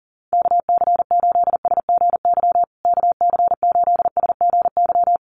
Morse code, heard as a keyed tone, about 40 wpm, 700 Hz. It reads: KC8SGY KC8SGY
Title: USA Call Signs Repeated 2x 40wpm